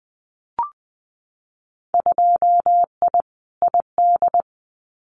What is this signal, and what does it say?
Morse code, heard as a keyed tone, about 20 wpm, 700 Hz. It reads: 2I ID
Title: Winter Field Day Exchanges Encore 20wpm